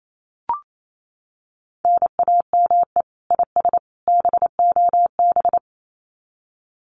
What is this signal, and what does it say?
Morse code, heard as a keyed tone, about 28 wpm, 700 Hz. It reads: NAME IS BOB
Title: Common QSO Phrases 28wpm